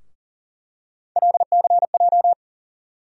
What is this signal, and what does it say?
Morse code, heard as a keyed tone, about 40 wpm, 700 Hz. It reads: LCJ
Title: USA Call Sign Postfixes - 3 Letters 40wpm